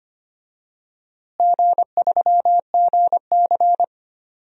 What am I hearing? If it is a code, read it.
Morse code, heard as a keyed tone, about 25 wpm, 700 Hz. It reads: G3GC
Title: DXCC-100 Call Signs - 4 Characters - Repeated 2x - 25wpm